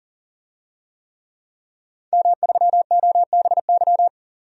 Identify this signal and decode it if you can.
Morse code, heard as a keyed tone, about 40 wpm, 700 Hz. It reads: M3OBY
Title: DXCC-100 Call Signs - 5 Characters - 40wpm